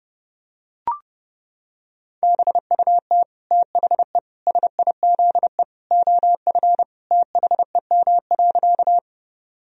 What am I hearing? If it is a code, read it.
Morse code, heard as a keyed tone, about 30 wpm, 700 Hz. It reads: BUT THE SIZE OF THEM.
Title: Sentences from Top 1000 Words - Review Part 2 - 30wpm